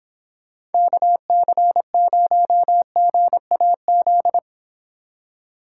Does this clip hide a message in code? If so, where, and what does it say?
Yes: Morse code from 0.7 to 4.4 s, heard as a keyed tone, about 29 wpm, 700 Hz. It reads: KC0GAZ